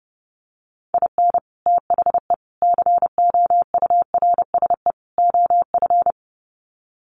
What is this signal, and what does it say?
Morse code, heard as a keyed tone, about 30 wpm, 700 Hz. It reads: IN THE COURSE OF
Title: Sets of 4 Words 30wpm